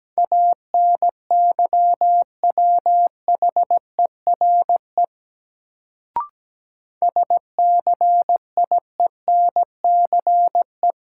Morse code, heard as a keyed tone, about 17 wpm, 700 Hz. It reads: ANYWHERE SCIENCE
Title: Sentences from Top 1400 Words - New Word Drill Alt - 17wpm